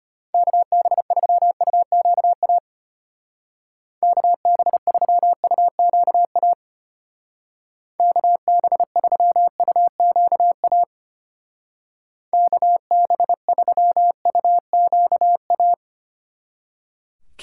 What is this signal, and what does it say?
Morse code, heard as a keyed tone, about 32 wpm, 700 Hz. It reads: KB3UQA KB3UQA KB3UQA KB3UQA